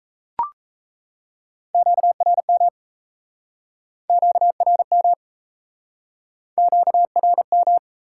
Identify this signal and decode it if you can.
Morse code, heard as a keyed tone, about 38 wpm, 700 Hz. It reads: QRM QRM QRM